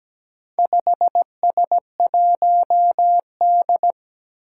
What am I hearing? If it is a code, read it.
Morse code, heard as a keyed tone, about 17 wpm, 700 Hz. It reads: 5S1D
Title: Four Challenging Letter-Number Combinations 17wpm